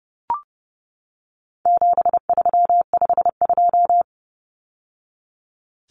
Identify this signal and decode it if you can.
Morse code, heard as a keyed tone, about 30 wpm, 700 Hz. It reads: 7352
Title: Numbers - 4-Digit 30wpm